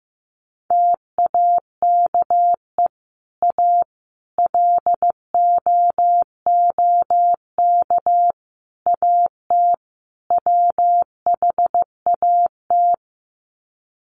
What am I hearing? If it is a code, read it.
Morse code, heard as a keyed tone, about 15 wpm, 700 Hz. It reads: TAKE A LOOK AT WHAT
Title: Sets of 5 Words Encore2 15wpm